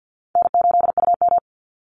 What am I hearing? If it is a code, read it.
Morse code, heard as a keyed tone, about 50 wpm, 700 Hz. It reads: N8UM
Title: Call Signs - 50wpm